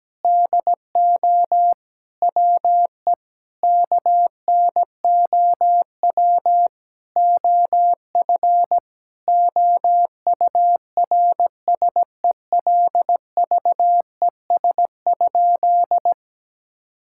Morse code, heard as a keyed tone, about 17 wpm, 700 Hz. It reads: DO WE KNOW OF OURSELVES?
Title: Sentences from Top 1600 Words 17wpm